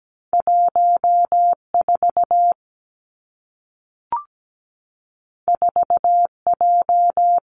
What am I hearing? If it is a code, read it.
Morse code, heard as a keyed tone, about 17 wpm, 700 Hz. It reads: 14 4J
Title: Two Challenging Letter-Number Combinations 17wpm